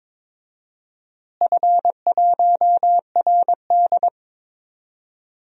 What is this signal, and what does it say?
Morse code, heard as a keyed tone, about 22 wpm, 700 Hz. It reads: F1RD
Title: DXCC-100 Call Signs - 4 Characters - Repeated 2x - 22wpm